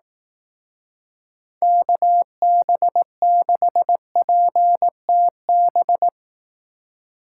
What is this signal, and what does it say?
Morse code, heard as a keyed tone, about 18 wpm, 700 Hz. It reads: KB6PTB
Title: Speed Racing - USA 2x3 Call Signs 15wpm